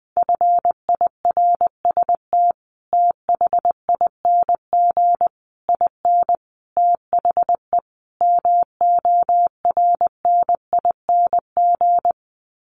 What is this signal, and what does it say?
Morse code, heard as a keyed tone, about 20 wpm, 700 Hz. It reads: FIRST THING IN THE MORNING
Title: Sets of 5 Words Encore2 20wpm